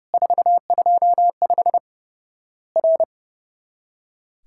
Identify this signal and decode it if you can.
Morse code, heard as a keyed tone, about 32 wpm, 700 Hz. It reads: 425 R
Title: Worked All Europe DX Contest - QTC Exchange - 30wpm